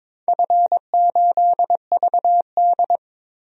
Morse code, heard as a keyed tone, about 22 wpm, 700 Hz. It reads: F8VD
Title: DXCC-100 Call Signs - 4 Characters - Repeated 3x - 22wpm